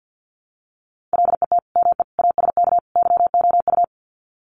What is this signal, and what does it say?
Morse code, heard as a keyed tone, about 50 wpm, 700 Hz. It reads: LET ME ASK YOU
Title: Sets of 4 Words 50wpm